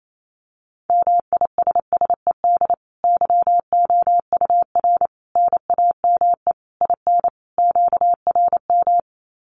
Morse code, heard as a keyed tone, about 28 wpm, 700 Hz. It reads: MISSED YOUR NAME IN QRM
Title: Common QSO Phrases 28wpm